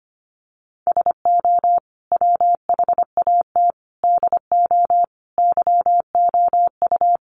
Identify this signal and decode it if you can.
Morse code, heard as a keyed tone, about 25 wpm, 700 Hz. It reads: SO WHAT DO YOU
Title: Sets of 5 Words Encore2 25wpm